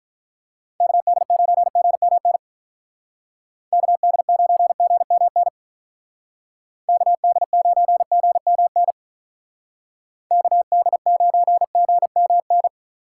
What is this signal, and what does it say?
Morse code, heard as a keyed tone, about 47 wpm, 700 Hz. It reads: KD9GMN KD9GMN KD9GMN KD9GMN